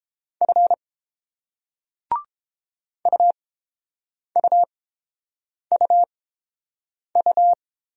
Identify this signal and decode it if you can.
Morse code, heard as a keyed tone, about 30 wpm, spaced as for 7 wpm, 690 Hz. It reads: F UUUU